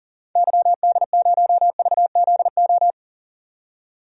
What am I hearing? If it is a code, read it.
Morse code, heard as a keyed tone, about 40 wpm, 700 Hz. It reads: YD0VZO
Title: DXCC-100 Call Signs - 6 Characters - 40wpm